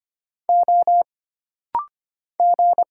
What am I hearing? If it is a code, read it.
Morse code, heard as a keyed tone, about 25 wpm, 700 Hz. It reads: O G
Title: Single Letters - Rapid-Fire 25wpm